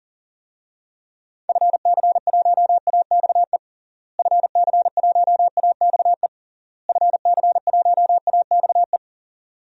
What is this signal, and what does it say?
Morse code, heard as a keyed tone, about 37 wpm, 700 Hz. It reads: FC1AXE FC1AXE FC1AXE
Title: DXCC-100 Call Signs - 6 Characters - Repeated 3x - 40wpm